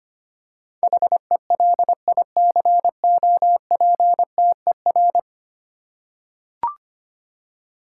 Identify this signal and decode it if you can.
Morse code, heard as a keyed tone, about 25 wpm, 700 Hz. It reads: HELICOPTER
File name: Ten Letter Words - 25wpm